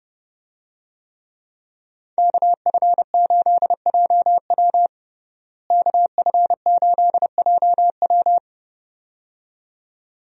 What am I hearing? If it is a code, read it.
Morse code, heard as a keyed tone, about 30 wpm, 700 Hz. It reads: KF8JW KF8JW
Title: USA Call Signs - 5 Characters - Repeated 2x - 30wpm